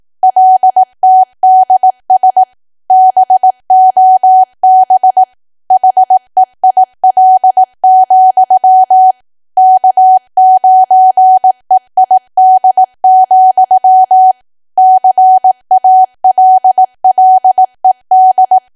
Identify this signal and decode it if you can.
Morse code, heard as a keyed tone, about 18 wpm, 750 Hz. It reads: LTDS BOB HEIL, K9EID, CALLED
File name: Archive / 18 WPM / 18WPM